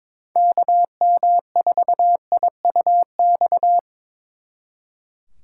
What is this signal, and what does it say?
Morse code, heard as a keyed tone, about 25 wpm, 700 Hz. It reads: KM4IUX
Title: Speed Racing - USA 2x3 Call Signs 22wpm